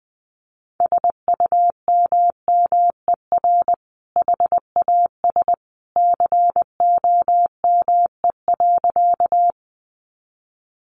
Morse code, heard as a keyed tone, about 19 wpm, 700 Hz. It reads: SUMMER HAS COME.
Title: Sentences from Top 1100 Words 20wpm